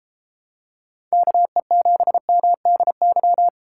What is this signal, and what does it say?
Morse code, heard as a keyed tone, about 32 wpm, 700 Hz. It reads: KE7MDY